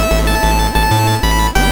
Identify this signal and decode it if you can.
There is no code.